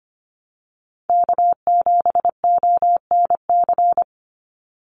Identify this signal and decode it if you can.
Morse code, heard as a keyed tone, about 25 wpm, 700 Hz. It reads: K7ONC
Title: USA Call Signs 25wpm